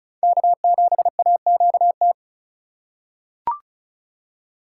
Morse code, heard as a keyed tone, about 34 wpm, 700 Hz. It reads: K7AQT
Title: USA Call Signs - 5 Characters - 4x Character Spacing - 35wpm